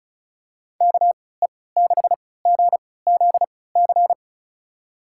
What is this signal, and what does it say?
Morse code, heard as a keyed tone, about 36 wpm, spaced as for 20 wpm, 700 Hz. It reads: KE6GZC
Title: USA Call Signs - 6 Characters - 3x Character Spacing - 35wpm